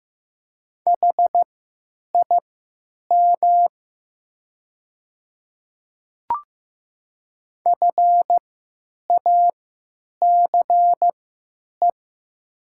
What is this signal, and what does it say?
Morse code, heard as a keyed tone, about 15 wpm, spaced as for 9 wpm, 700 Hz. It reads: HIM FACE
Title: 3x Character Spacing - Top 300 Words - 15wpm